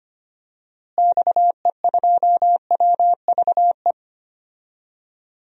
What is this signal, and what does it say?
Morse code, heard as a keyed tone, about 25 wpm, 700 Hz. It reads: XE2WVE
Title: DXCC-100 Call Signs - 6 Characters - Repeated 2x - 25wpm